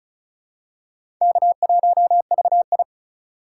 Morse code, heard as a keyed tone, about 36 wpm, 700 Hz. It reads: K1VI
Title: USA Call Signs - 4 Characters - 2x Character Spacing - 35wpm